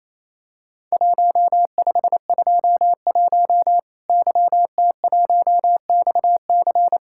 Morse code, heard as a keyed tone, about 28 wpm, 700 Hz. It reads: 1521 YT1XC
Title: Worked All Europe DX Contest - QTC Exchange - 28wpm